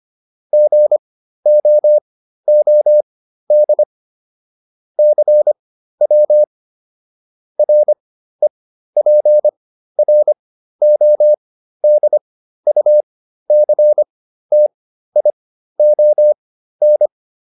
Morse code, heard as a keyed tone, about 25 wpm, spaced as for 13 wpm, 600 Hz. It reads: GOOD CW REPRODUCTION